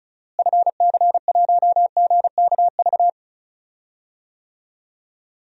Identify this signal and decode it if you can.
Morse code, heard as a keyed tone, about 36 wpm, 700 Hz. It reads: FC1GKV
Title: DXCC-100 Call Signs - 6 Characters - Repeated 2x - 35wpm